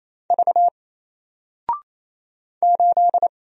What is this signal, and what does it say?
Morse code, heard as a keyed tone, about 28 wpm, 700 Hz. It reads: V 8
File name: Single-Letter-Number-28wpm